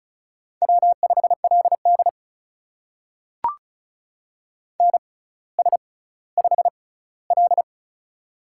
Morse code, heard as a keyed tone, about 36 wpm, spaced as for 18 wpm, 700 Hz. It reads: W5LD NS5L
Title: USA Call Signs - 4 Characters - 6x Character Spacing - 35wpm